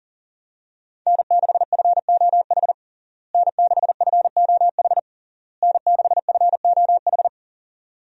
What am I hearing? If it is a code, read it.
Morse code, heard as a keyed tone, about 37 wpm, 700 Hz. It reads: N6FOH N6FOH N6FOH
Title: USA Call Signs - 5 Characters - Repeated 3x - 40wpm